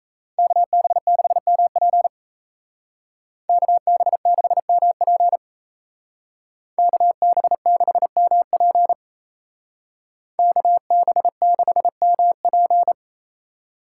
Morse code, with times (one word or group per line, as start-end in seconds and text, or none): 0.4-2.1 KB6MP
3.5-5.4 KB6MP
6.8-8.9 KB6MP
10.4-12.9 KB6MP